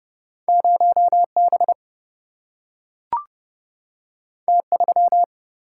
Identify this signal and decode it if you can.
Morse code, heard as a keyed tone, about 30 wpm, 700 Hz. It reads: 0 B T 3
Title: Single-Letter Single-Number 30wpm